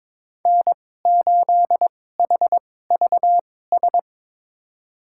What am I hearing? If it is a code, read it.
Morse code, heard as a keyed tone, about 22 wpm, spaced as for 16 wpm, 700 Hz. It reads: N8HVS
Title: USA Call Signs - 5 Characters - 2x Character Spacing - 22wpm